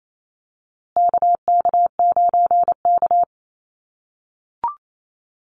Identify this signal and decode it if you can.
Morse code, heard as a keyed tone, about 28 wpm, 700 Hz. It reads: KK9K
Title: USA Call Signs Repeated 3x 28wpm